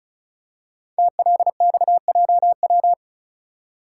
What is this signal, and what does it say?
Morse code, heard as a keyed tone, about 36 wpm, 700 Hz. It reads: TLXJW
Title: Code Groups - 5 Letters - 35wpm